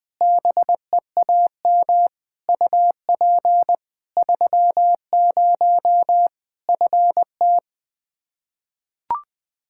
Morse code, heard as a keyed tone, about 20 wpm, 700 Hz. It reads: BEAM UP 30 FT
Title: Practice QSO - Exchange 2 - Info - 20wpm